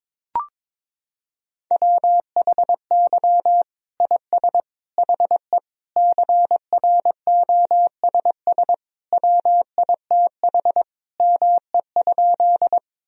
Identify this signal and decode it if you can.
Morse code, heard as a keyed tone, about 22 wpm, 700 Hz. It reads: WHY IS HE CROSS WITH ME?
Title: Sentences from Top 1000 Words - Review Part 1 - 22wpm